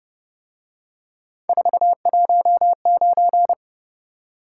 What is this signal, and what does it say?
Morse code, heard as a keyed tone, about 28 wpm, 700 Hz. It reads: 419